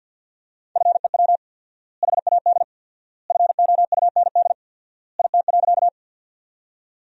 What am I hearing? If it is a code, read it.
Morse code, heard as a keyed tone, about 53 wpm, 700 Hz. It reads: FEW HAD FOUND IT.
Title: 4x Word-Spacing - Sentences from Top 300 Words - 50wpm